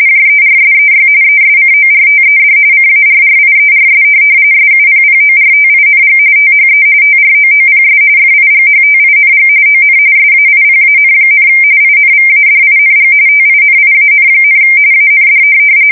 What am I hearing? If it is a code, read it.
RTTY, about 45.45 baud. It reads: JERSEY ESCORTED CARRIERS SUPPORTING THE TROOPS FIGHTING ON SAIPAN, INVADING TINIAN, AND
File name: fsk170 / USS-New-Jersey